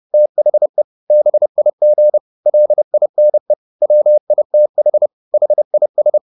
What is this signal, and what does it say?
Morse code, heard as a keyed tone, about 30 wpm, 600 Hz. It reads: THE BIG LINE WITH HIS